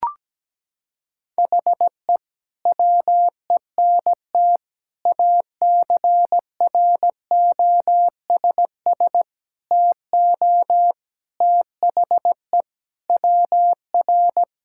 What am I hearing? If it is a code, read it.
Morse code, heard as a keyed tone, about 17 wpm, 700 Hz. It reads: HE WENT ACROSS TO THE WR
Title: Sentences from Top 1000 Words - Review Part 5 - 17wpm